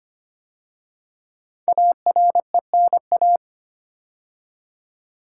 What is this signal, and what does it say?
Morse code, heard as a keyed tone, about 25 wpm, 700 Hz. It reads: ARENA
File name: 5 Letter Words - First 3 Letters Given - 25wpm